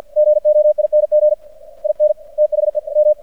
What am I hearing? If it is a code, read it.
Morse code, heard as a keyed tone, about 52 wpm, 600 Hz. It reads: GOING IN THE M
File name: archived / CRSnet